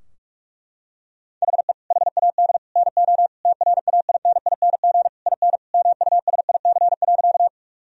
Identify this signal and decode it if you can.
Morse code, heard as a keyed tone, about 41 wpm, 700 Hz. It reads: HE HAD NO TRAINING IN MUSIC.
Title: Sentences from Top 1000 Words - Review Part 4 - 45wpm